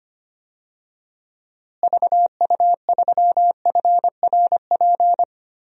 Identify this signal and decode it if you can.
Morse code, heard as a keyed tone, about 25 wpm, 700 Hz. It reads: VU3FRP
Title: DXCC-100 Call Signs - 6 Characters - Repeated 3x - 25wpm